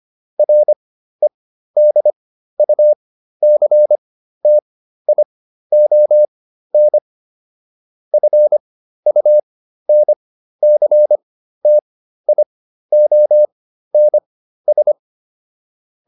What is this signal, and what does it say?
Morse code, heard as a keyed tone, about 25 wpm, spaced as for 13 wpm, 600 Hz. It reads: REDUCTION FUNCTIONS